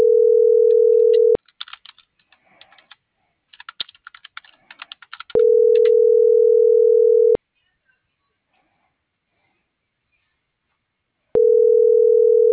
Ringback tone (North America), 440 + 480 Hz.
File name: CENTRAL MULTIMODAL / out josimar.cruz